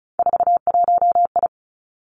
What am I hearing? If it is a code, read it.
Morse code, heard as a keyed tone, about 36 wpm, 700 Hz. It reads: 41I
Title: Three Challenging Letter-Number Combinations 35wpm